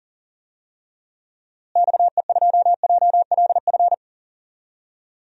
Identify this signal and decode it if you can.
Morse code, heard as a keyed tone, about 40 wpm, 700 Hz. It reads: XE2JLF
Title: DXCC-100 Call Signs - 6 Characters - Repeated 3x - 40wpm